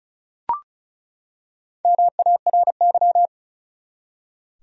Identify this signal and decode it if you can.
Morse code, heard as a keyed tone, about 34 wpm, 700 Hz. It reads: MARY
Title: USA Common Names - 35wpm